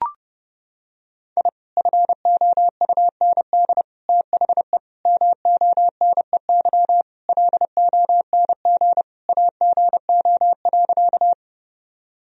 Morse code, heard as a keyed tone, about 30 wpm, 700 Hz. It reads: I FOUND THE MONEY LONG AGO.
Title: Sentences from Top 1000 Words - Review Part 3 - 30wpm